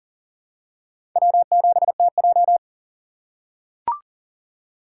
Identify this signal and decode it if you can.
Morse code, heard as a keyed tone, about 38 wpm, 700 Hz. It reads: W7TJ
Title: USA Call Signs - 4 Characters - 3x Character Spacing - 40wpm